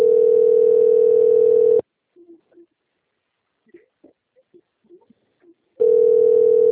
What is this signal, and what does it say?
Ringback tone (North America), 440 + 480 Hz.
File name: out- - - - - .153